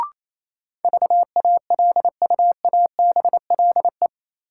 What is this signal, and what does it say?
Morse code, heard as a keyed tone, about 28 wpm, 700 Hz. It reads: VALUABLE
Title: Sentences from Top 1100 Words - New Word Drill - 28wpm